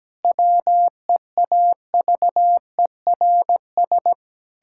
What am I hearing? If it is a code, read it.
Morse code, heard as a keyed tone, about 17 wpm, 700 Hz. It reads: WEAVERS
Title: Pangram Challenge v2 - 17wpm